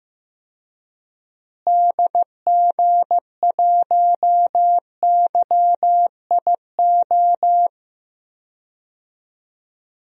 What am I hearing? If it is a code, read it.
Morse code, heard as a keyed tone, about 15 wpm, 700 Hz. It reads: DG1YIO
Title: DXCC-100 Call Signs - 6 Characters - Repeated 2x - 15wpm